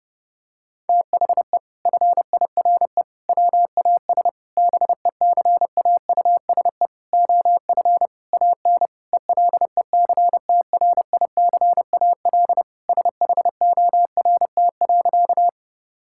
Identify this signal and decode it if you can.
Morse code, heard as a keyed tone, about 30 wpm, 700 Hz. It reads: THE FIRE WAS BECAUSE OF AN ELECTRICAL SHORT.
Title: Sentences from Top 1000 Words - Review Part 6 - 30wpm